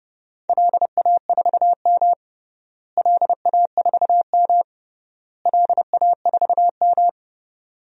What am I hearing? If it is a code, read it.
Morse code, heard as a keyed tone, about 32 wpm, 700 Hz. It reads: LA4M LA4M LA4M
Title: DXCC-100 Call Signs - 4 Characters - Repeated 3x - 30wpm